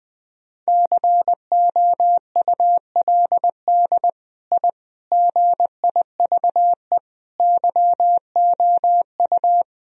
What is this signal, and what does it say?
Morse code, heard as a keyed tone, about 21 wpm, 700 Hz. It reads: COULD I GIVE YOU
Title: Sentences from Top 1000 Words - Review Part 1 - 20wpm